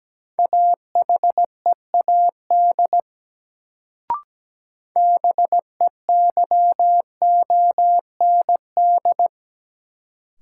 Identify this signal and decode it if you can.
Morse code, heard as a keyed tone, about 17 wpm, 700 Hz. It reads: AHEAD BEYOND
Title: Sentences from Top 1100 Words - New Word Drill Alt - 17wpm